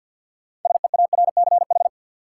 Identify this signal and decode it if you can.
Morse code, heard as a keyed tone, about 52 wpm, 700 Hz. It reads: SEARCH
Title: Sentences from Top 800 Words - New Word Drill - 50wpm